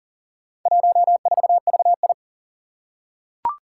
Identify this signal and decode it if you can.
Morse code, heard as a keyed tone, about 38 wpm, 700 Hz. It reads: 14VI
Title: Four Challenging Letter-Number Combinations 40wpm